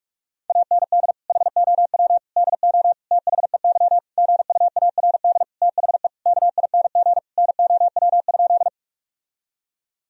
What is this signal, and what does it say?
Morse code, heard as a keyed tone, about 46 wpm, 700 Hz. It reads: AND HOW DO THEY GUARD THE KING NOW?
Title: Sentences from Top 1000 Words - Review Part 4 - 45wpm